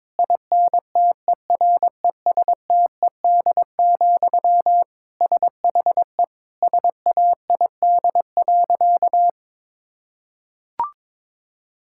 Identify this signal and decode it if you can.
Morse code, heard as a keyed tone, about 22 wpm, 700 Hz. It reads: INTERESTED, SHE SAID.
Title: Sentences from Top 1000 Words - Review Part 6 - 22wpm